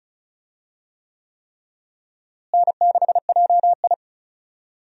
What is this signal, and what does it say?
Morse code, heard as a keyed tone, about 34 wpm, 700 Hz. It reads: N6JI
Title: USA Call Signs - 4 Characters - Repeated 3x - 35wpm